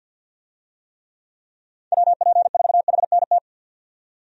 Morse code, heard as a keyed tone, about 46 wpm, 700 Hz. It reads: WP4HNT
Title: USA Call Signs - 6 Characters - 50wpm